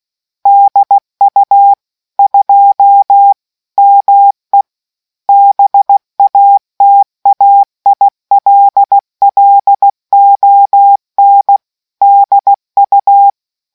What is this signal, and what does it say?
Morse code, heard as a keyed tone, about 16 wpm, 800 Hz. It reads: DU 2 ME BATAILLON DU